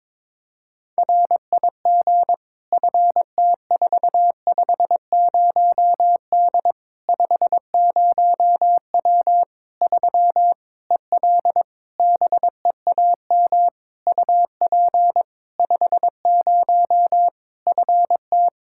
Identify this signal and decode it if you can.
Morse code, heard as a keyed tone, about 22 wpm, 700 Hz. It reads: RIG FT450D 50W 3 EL BEAM UP 50 FT
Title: K1USN SST QSO Practice 22wpm